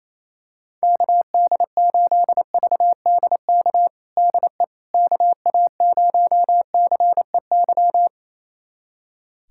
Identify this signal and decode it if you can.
Morse code, heard as a keyed tone, about 28 wpm, 700 Hz. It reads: KD8VDK DE KA0CEY
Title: USA Callsign DE Callsign 28wpm